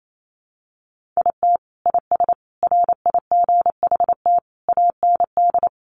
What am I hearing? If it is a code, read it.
Morse code, heard as a keyed tone, about 28 wpm, 700 Hz. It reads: IT IS RIGHT AND
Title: Sentences from Top 1500 Words 28wpm